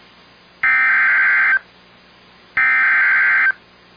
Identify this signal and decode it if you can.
Emergency Alert System (SAME) tones (header bursts), heard as harsh buzzing bursts. It is heard from 0.6 to 3.5 s.